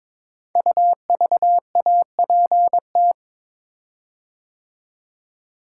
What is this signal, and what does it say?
Morse code, heard as a keyed tone, about 22 wpm, 700 Hz. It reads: UVAPT
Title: Code Groups - 5 Letter:Numbers - 22wpm